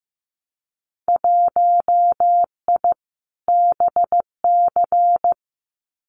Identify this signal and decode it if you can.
Morse code, heard as a keyed tone, about 15 wpm, 700 Hz. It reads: 1I BC
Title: Winter Field Day Exchanges Encore 15wpm